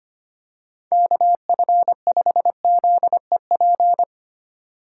Morse code, heard as a keyed tone, about 25 wpm, 700 Hz. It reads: KF5ZEP
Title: USA Call Signs - 6 Characters - 4x Character Spacing - 25wpm